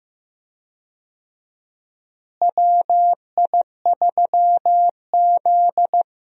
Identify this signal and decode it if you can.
Morse code, heard as a keyed tone, about 15 wpm, 700 Hz. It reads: WI3Z
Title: USA Call Signs - 4 Characters - Repeated 3x - 15wpm